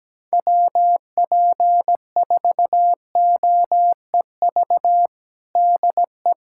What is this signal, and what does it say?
Morse code, heard as a keyed tone, about 17 wpm, 700 Hz. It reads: WP4OEV DE
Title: USA Callsign DE Callsign - 6 Characters - 17wpm